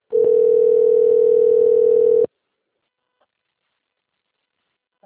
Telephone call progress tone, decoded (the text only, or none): ringback tone